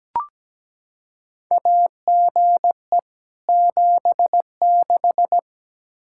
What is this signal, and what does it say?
Morse code, heard as a keyed tone, about 17 wpm, 700 Hz. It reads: AGE 76
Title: K1USN SST QSO Practice 17wpm